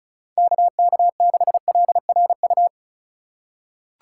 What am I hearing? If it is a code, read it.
Morse code, heard as a keyed tone, about 36 wpm, 700 Hz. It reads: KK6LRU
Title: USA Call Signs - 6 Characters - 4x Character Spacing - 35wpm